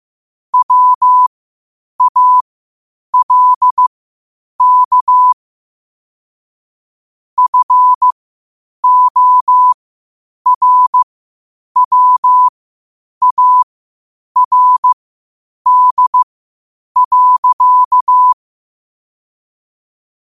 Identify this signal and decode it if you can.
Morse code, heard as a keyed tone, about 15 wpm, spaced as for 8 wpm, 1000 Hz. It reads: WALK FORWARD.
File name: qotdfiles / 8wpm